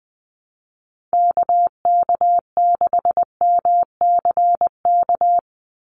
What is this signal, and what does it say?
Morse code, heard as a keyed tone, about 20 wpm, 700 Hz. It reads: KK6MCK
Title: USA Call Signs 20wpm